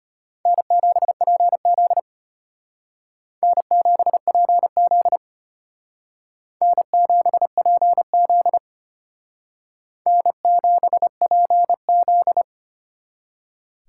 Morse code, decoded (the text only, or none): N7PZ N7PZ N7PZ N7PZ